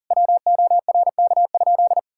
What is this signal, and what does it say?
Morse code, heard as a keyed tone, about 40 wpm, 700 Hz. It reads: WORK?
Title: Sentences from Top 1000 Words - Review Part 5 - 40wpm